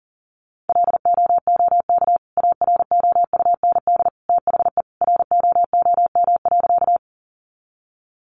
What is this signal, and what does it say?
Morse code, heard as a keyed tone, about 40 wpm, 700 Hz. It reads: LOOK AROUND THE ROOM.
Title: Sentences from Top 600 Words 40wpm